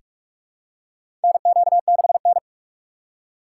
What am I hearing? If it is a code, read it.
Morse code, heard as a keyed tone, about 46 wpm, 700 Hz. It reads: NQ6N
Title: USA Call Signs - 3x Character Spacing - 45wpm